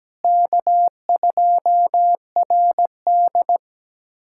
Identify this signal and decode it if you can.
Morse code, heard as a keyed tone, about 17 wpm, 700 Hz. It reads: K2RD
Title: K1USN SST QSO Practice - 17wpm